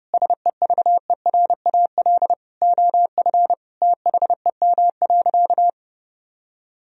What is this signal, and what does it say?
Morse code, heard as a keyed tone, about 30 wpm, 700 Hz. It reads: SEVERAL OF THEM.
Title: Sentences from Top 1000 Words - Review Part 2 - 30wpm